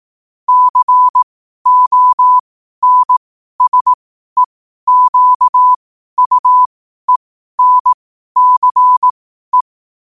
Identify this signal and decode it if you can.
Morse code, heard as a keyed tone, about 18 wpm, spaced as for 12 wpm, 1000 Hz. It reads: CONSEQUENCE
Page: New quotes every day in morse code at 12 Words per minute.